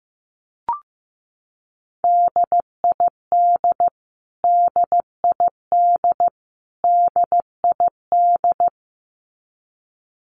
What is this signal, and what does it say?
Morse code, heard as a keyed tone, about 15 wpm, 700 Hz. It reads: DID DID DID
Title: Top 200 Words Repeated 15wpm